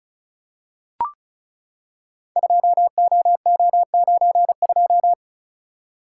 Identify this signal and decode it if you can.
Morse code, heard as a keyed tone, about 35 wpm, 700 Hz. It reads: 2OO92
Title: Code Groups - 5 Letter:Numbers - 35wpm